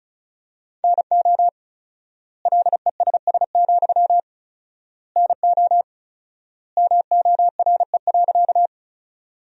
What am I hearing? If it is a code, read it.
Morse code, heard as a keyed tone, about 34 wpm, 700 Hz. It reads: NO LESS, NO MORE.
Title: 4x Word-Spacing - Sentences from Top 400 Words - 35wpm